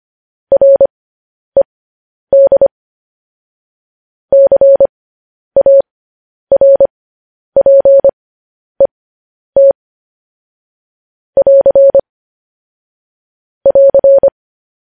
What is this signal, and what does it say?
Morse code, heard as a keyed tone, about 25 wpm, spaced as for 10 wpm, 550 Hz. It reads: RED CARPET <AR> <AR>